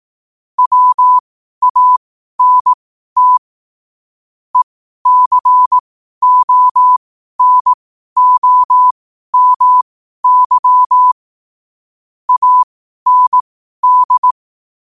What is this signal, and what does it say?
Morse code, heard as a keyed tone, about 18 wpm, spaced as for 12 wpm, 1000 Hz. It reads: WANT ECONOMY AND
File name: qotdfiles / 12wpm